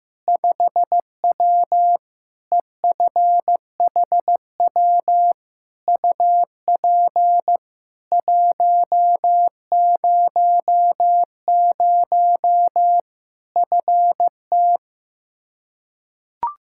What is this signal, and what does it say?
Morse code, heard as a keyed tone, about 15 wpm, 700 Hz. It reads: 5W EFHW UP 100 FT
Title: Practice QSO - Exchange 2 - Info - 15wpm